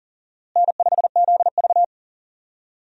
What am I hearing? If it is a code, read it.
Morse code, heard as a keyed tone, about 40 wpm, 700 Hz. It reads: N5ZV
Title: USA Call Signs - 4 Characters - Repeated 3x - 40wpm